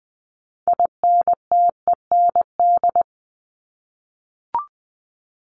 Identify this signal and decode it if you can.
Morse code, heard as a keyed tone, about 20 wpm, 700 Hz. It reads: INTEND
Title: Top 1501-1750 Words 20wpm